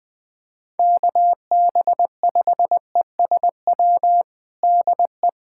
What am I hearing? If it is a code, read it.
Morse code, heard as a keyed tone, about 21 wpm, 700 Hz. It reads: KB5ESW DE
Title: USA Callsign DE Callsign 20wpm